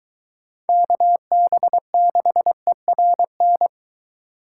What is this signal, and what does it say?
Morse code, heard as a keyed tone, about 23 wpm, 700 Hz. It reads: KB6ERN